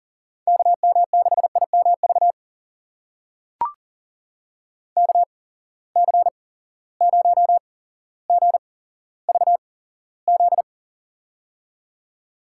Morse code, heard as a keyed tone, about 40 wpm, spaced as for 17 wpm, 700 Hz. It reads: KM6IMV KC0GVZ
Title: USA Call Signs - 6 Characters - 8x Character Spacing - 40wpm